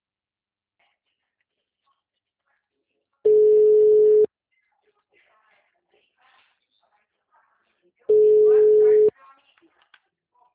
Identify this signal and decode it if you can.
Ringback tone (Europe, 425 Hz), with speech.